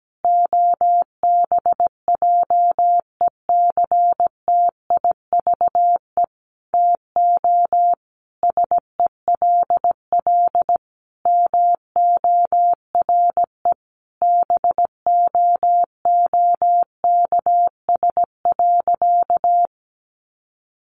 Morse code, heard as a keyed tone, about 17 wpm, 700 Hz. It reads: OBJECTIVE TO SELL MORE BOOKS.
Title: Sentences from Top 1200 Words 17wpm